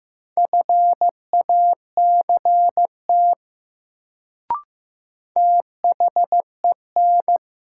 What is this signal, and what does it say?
Morse code, heard as a keyed tone, about 15 wpm, 700 Hz. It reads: FACT THEN
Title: Sets of 2 Words - New Word Drill - 15wpm